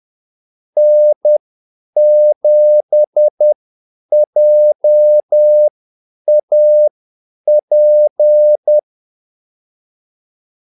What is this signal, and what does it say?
Morse code, heard as a keyed tone, about 10 wpm, spaced as for 8 wpm, 600 Hz. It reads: N7JAP
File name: Callsigns-10-8wpm